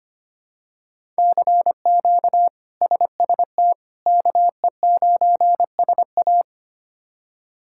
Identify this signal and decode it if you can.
Morse code, heard as a keyed tone, about 25 wpm, 700 Hz. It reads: CQ SST KE9SA
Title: K1USN SST QSO Practice - 25wpm